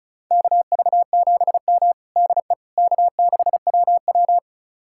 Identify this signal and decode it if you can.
Morse code, heard as a keyed tone, about 34 wpm, 700 Hz. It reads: KV7M DE K6WW
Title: USA Callsign DE Callsign - 4 Characters - 35wpm